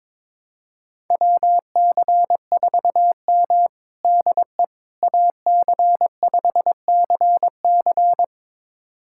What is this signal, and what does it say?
Morse code, heard as a keyed tone, about 22 wpm, 700 Hz. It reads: WC4M DE AC5CC
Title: USA Callsign DE Callsign 22wpm